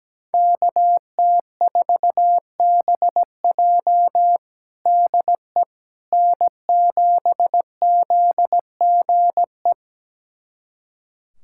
Morse code, heard as a keyed tone, about 17 wpm, 700 Hz. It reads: KT4BJ DE N7ZGE
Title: USA Callsign DE Callsign - 5 Characters - 17wpm